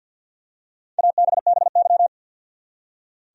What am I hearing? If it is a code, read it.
Morse code, heard as a keyed tone, about 51 wpm, 700 Hz. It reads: ABBY
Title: USA Common Names - Max 4 Letters - 50wpm